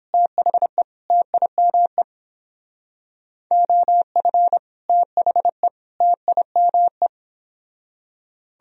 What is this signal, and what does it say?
Morse code, heard as a keyed tone, about 30 wpm, 700 Hz. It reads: THE TIME OF THE TIME